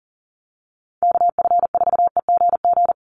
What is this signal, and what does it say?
Morse code, heard as a keyed tone, about 40 wpm, 700 Hz. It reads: KF4EGG
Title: USA Call Signs 40wpm